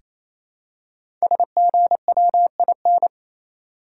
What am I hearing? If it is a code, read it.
Morse code, heard as a keyed tone, about 28 wpm, 700 Hz. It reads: SGWIN
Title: Code Groups - 5 Letters - 28wpm